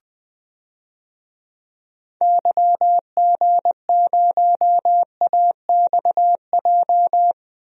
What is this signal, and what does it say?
Morse code, heard as a keyed tone, about 19 wpm, 700 Hz. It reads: YG0AXJ
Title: DXCC-100 Call Signs - 6 Characters - Repeated 3x - 20wpm